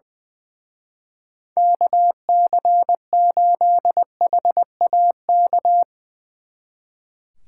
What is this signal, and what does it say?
Morse code, heard as a keyed tone, about 22 wpm, 700 Hz. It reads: KC8HAK
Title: Speed Racing - USA Call Signs 20wpm